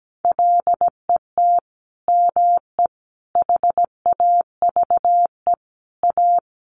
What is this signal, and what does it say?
Morse code, heard as a keyed tone, about 17 wpm, 700 Hz. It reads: LET ME HAVE A
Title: Sentences from Top 300 Words 17wpm